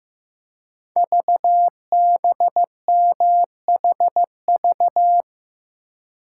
Morse code, heard as a keyed tone, about 15 wpm, 700 Hz. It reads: VBMHV
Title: Code Groups - 5 Letters - 15wpm